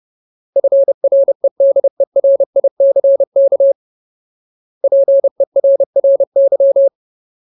Morse code, heard as a keyed tone, about 30 wpm, 550 Hz. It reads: FREDERICK PERRY
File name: Ultimate Word List / 30WPM Ultimate